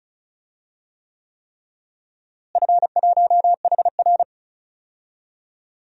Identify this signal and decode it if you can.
Morse code, heard as a keyed tone, about 36 wpm, 700 Hz. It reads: F1HR
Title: DXCC-100 Call Signs - 4 Characters - Repeated 3x - 35wpm